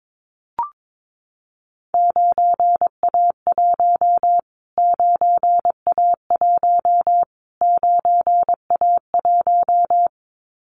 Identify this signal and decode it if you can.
Morse code, heard as a keyed tone, about 22 wpm, 700 Hz. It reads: 9A1 9A1 9A1
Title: Callsign Prefixes for Top 25 DXCC - Repeated 3x - 22wpm